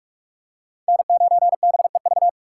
Morse code, heard as a keyed tone, about 49 wpm, 700 Hz. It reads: N9BEV